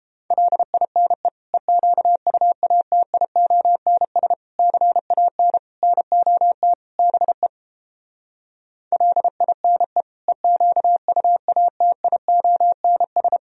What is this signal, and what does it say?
Morse code, heard as a keyed tone, about 29 wpm, 700 Hz. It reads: LINE EQUATIONS CAN NOT BE LINE EQUATIONS